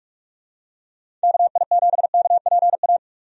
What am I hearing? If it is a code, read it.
Morse code, heard as a keyed tone, about 45 wpm, 700 Hz. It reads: KI7KPA
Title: USA Call Signs - 6 Characters - 45wpm